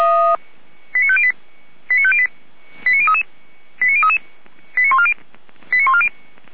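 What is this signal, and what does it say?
Five-tone selective call (ZVEI), heard as quick runs of five tones, tones of about 70 ms. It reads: 89508 89508 89033 89033 89140 89140
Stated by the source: ZVEI five-tone selective call received by SDR.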